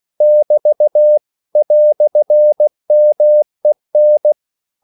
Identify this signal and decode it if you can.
Morse code, heard as a keyed tone, about 16 wpm, 600 Hz. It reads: <BT> "MEN